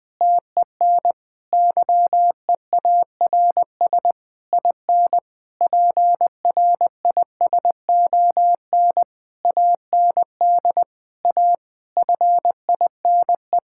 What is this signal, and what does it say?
Morse code, heard as a keyed tone, about 21 wpm, 700 Hz. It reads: TEN YEARS IN PRISON AND A FINE
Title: Sentences with 9-Word Phrases - 20wpm